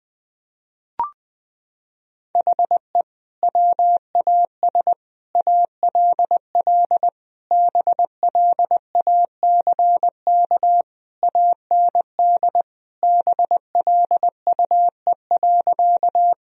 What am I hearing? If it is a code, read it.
Morse code, heard as a keyed tone, about 20 wpm, 700 Hz. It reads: HE WAS ALL BLACK AND BLUE.
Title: Sentences from Top 1000 Words - Review Part 3 - 20wpm